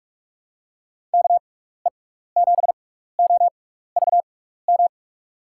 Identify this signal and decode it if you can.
Morse code, heard as a keyed tone, about 41 wpm, spaced as for 15 wpm, 700 Hz. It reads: KE7OVM
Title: USA Call Signs - 6 Characters - 6x Character Spacing - 45wpm